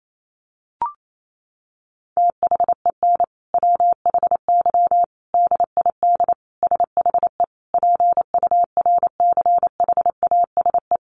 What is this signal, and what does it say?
Morse code, heard as a keyed tone, about 28 wpm, 700 Hz. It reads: THEN WHY DID SHE PURCHASE
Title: Sentences from Top 800 Words 28wpm